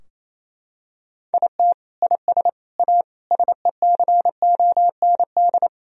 Morse code, heard as a keyed tone, about 28 wpm, 700 Hz. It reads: IT IS A SECOND
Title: Sentences from Top 1000 Words - Review Part 2 - 28wpm